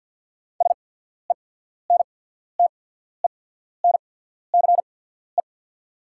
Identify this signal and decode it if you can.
Morse code, heard as a keyed tone, about 49 wpm, spaced as for 14 wpm, 690 Hz. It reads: SENTENCE
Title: 8x Character Spacing - Top 200 Words - 50wpm